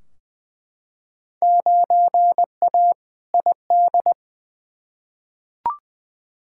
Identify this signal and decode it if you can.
Morse code, heard as a keyed tone, about 20 wpm, 700 Hz. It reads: 9A ID
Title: Field Day Exchanges Encore - 20wpm